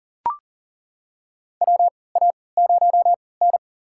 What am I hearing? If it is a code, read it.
Morse code, heard as a keyed tone, about 40 wpm, spaced as for 23 wpm, 700 Hz. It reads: WA0N
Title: USA Call Signs - 3x Character Spacing - 40wpm